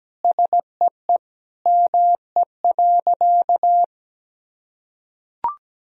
Morse code, heard as a keyed tone, about 17 wpm, 700 Hz. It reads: SEE ME.
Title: Sentences from Top 1000 Words - Review Part 1 - 17wpm